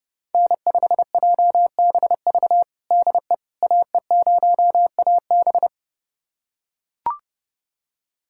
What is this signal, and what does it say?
Morse code, heard as a keyed tone, about 32 wpm, 700 Hz. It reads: N5JBV DE AE0AB
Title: USA Callsign DE Callsign - 5 Characters - 30wpm